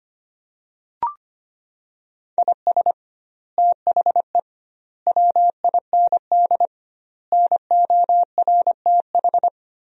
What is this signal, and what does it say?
Morse code, heard as a keyed tone, about 25 wpm, 700 Hz. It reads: IS THE WIND NORTH
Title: 2x Word-Spacing - Sentences from Top 400 Words - 25wpm